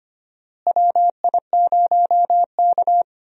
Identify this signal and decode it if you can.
Morse code, heard as a keyed tone, about 25 wpm, 700 Hz. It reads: WI0K
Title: USA Call Signs - 4 Characters - Repeated 2x - 25wpm